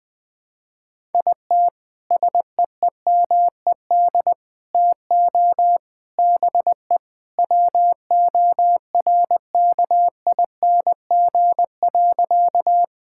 Morse code, heard as a keyed tone, about 21 wpm, 700 Hz. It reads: IT SEEMED TO BE WORKING.
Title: Sentences with 3-Word Phrases - 20wpm